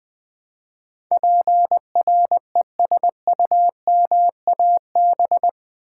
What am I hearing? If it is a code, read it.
Morse code, heard as a keyed tone, about 20 wpm, 700 Hz. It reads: PRESUMAB
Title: Build A Word - 4 Letters At A Time - 10-Letter Words - 20wpm